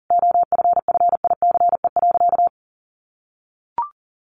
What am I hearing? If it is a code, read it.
Morse code, heard as a keyed tone, about 40 wpm, 700 Hz. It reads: OFFICE.
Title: Additional Easy Phrases 40wpm